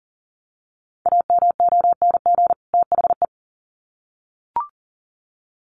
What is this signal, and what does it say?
Morse code, heard as a keyed tone, about 40 wpm, 700 Hz. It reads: AMONG THE
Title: Sets of 2 Words 40wpm